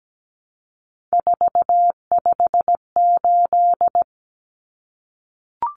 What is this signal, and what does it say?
Morse code, heard as a keyed tone, about 17 wpm, 700 Hz. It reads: 458
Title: Common Cut Numbers - 3-Digit 17wpm